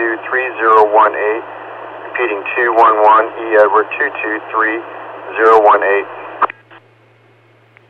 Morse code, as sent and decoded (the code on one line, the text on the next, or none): none